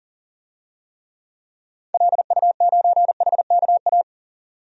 Morse code, heard as a keyed tone, about 38 wpm, 700 Hz. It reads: LU9HKA
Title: DXCC-100 Call Signs - 6 Characters - 40wpm